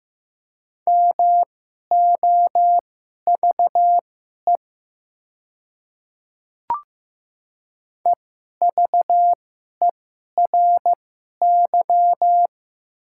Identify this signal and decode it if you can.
Morse code, heard as a keyed tone, about 15 wpm, spaced as for 11 wpm, 700 Hz. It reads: MOVE EVERY
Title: 2x Character Spacing - Top 500 Words - 15wpm